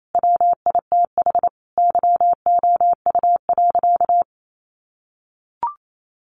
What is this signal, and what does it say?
Morse code, heard as a keyed tone, about 28 wpm, 700 Hz. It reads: WITH YOU.
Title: Sentences from Top 500 Words 28wpm